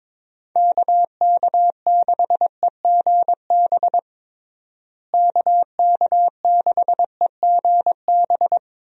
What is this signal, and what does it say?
Morse code, heard as a keyed tone, about 22 wpm, 700 Hz. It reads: KK6EGB KK6EGB
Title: USA Call Signs - 6 Characters - Repeated 3x - 22wpm